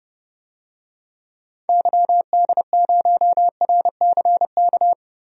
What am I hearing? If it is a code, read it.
Morse code, heard as a keyed tone, about 30 wpm, 700 Hz. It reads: YD0RCK
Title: DXCC-100 Call Signs - 6 Characters - Repeated 3x - 30wpm